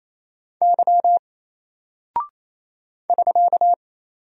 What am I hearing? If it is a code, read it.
Morse code, heard as a keyed tone, about 28 wpm, 700 Hz. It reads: Y <SK>
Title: Single Character (Letter:Number:Punctuation:Prosign) Alt 28wpm